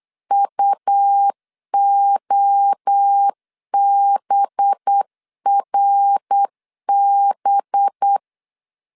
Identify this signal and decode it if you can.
Morse code, heard as a keyed tone, about 8 wpm, 800 Hz. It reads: UOBRB
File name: 8wpm